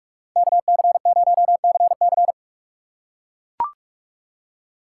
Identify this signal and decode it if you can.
Morse code, heard as a keyed tone, about 42 wpm, 700 Hz. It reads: KC0CC
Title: USA Call Signs - 5 Characters - 3x Character Spacing - 45wpm